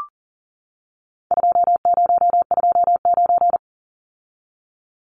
Morse code, heard as a keyed tone, about 40 wpm, 700 Hz. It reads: 2029
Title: Numbers - 4-Digit 40wpm